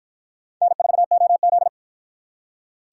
Morse code, heard as a keyed tone, about 47 wpm, 700 Hz. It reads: N4OZ